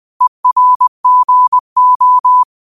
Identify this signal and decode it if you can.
Morse code, heard as a keyed tone, about 20 wpm, 1000 Hz. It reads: ERGO
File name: qotdfiles / 20wpm